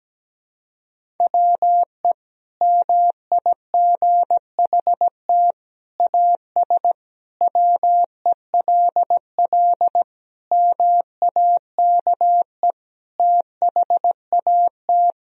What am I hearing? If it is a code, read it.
Morse code, heard as a keyed tone, about 17 wpm, 700 Hz. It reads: WE MIGHT AS WELL MAKE THAT
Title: Sentences from Top 1000 Words - Review Part 1 - 17wpm